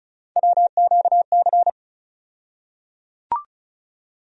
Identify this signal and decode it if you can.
Morse code, heard as a keyed tone, about 35 wpm, 700 Hz. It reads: WQC
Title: USA Call Sign Postfixes - 3 Letters 35wpm